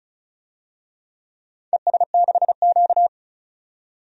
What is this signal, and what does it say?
Morse code, heard as a keyed tone, about 36 wpm, 700 Hz. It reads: ES6Q
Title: DXCC-100 Call Signs - 4 Characters - 35wpm